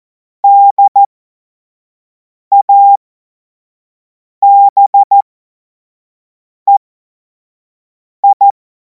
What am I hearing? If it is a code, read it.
Morse code, heard as a keyed tone, about 14 wpm, spaced as for 5 wpm, 800 Hz. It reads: DABEI
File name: Morsekurs / Witze / Witz 03. März 2026 25 F70